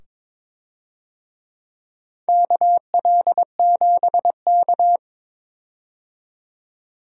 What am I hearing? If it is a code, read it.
Morse code, heard as a keyed tone, about 23 wpm, 700 Hz. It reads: KL7K
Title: DXCC-100 Call Signs - 4 Characters - 22wpm